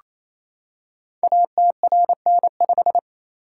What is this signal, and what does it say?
Morse code, heard as a keyed tone, about 29 wpm, 700 Hz. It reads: ATRN5
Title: Code Groups - 5 Letter:Numbers - 28wpm